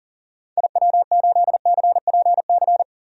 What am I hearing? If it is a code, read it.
Morse code, heard as a keyed tone, about 40 wpm, 700 Hz. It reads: IW8CPC
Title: DXCC-100 Call Signs - 6 Characters - 40wpm